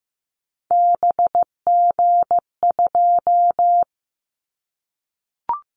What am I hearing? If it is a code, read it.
Morse code, heard as a keyed tone, about 15 wpm, 700 Hz. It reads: BG2
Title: Callsign Prefixes for Top 100 DXCC - 15wpm